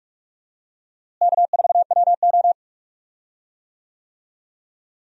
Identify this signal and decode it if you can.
Morse code, heard as a keyed tone, about 42 wpm, 700 Hz. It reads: K4WO
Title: USA Call Signs - 4 Characters - Repeated 2x - 45wpm